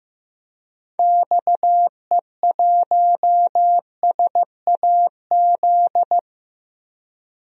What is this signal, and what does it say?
Morse code, heard as a keyed tone, about 15 wpm, 700 Hz. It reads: XE1SAZ
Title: DXCC-100 Call Signs - 6 Characters - 15wpm